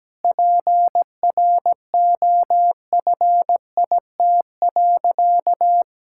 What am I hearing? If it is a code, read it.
Morse code, heard as a keyed tone, about 17 wpm, 700 Hz. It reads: PROFIT.
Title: Sentences from Top 1000 Words - Review Part 5 - 17wpm